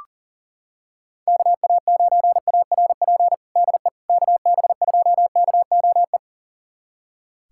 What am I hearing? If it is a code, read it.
Morse code, heard as a keyed tone, about 37 wpm, 700 Hz. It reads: KA9ARP DE KB2KOE
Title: USA Callsign DE Callsign - 6 Characters - 40wpm